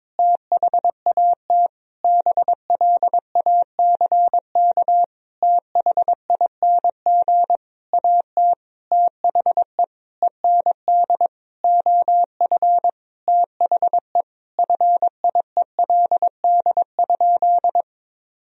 Morse code, heard as a keyed tone, about 21 wpm, 700 Hz. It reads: THAT BLACK THING AT THE END OF THE FIELD?
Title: Sentences from Top 1000 Words - Review Part 2 - 22wpm